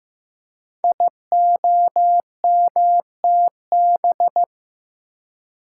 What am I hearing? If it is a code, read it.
Morse code, heard as a keyed tone, about 15 wpm, 700 Hz. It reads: IOMTB
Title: Code Groups - 5 Letters - 15wpm